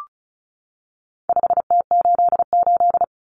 Morse code, heard as a keyed tone, about 34 wpm, 700 Hz. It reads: 5T88
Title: Common Cut Numbers - 4-Digit 35wpm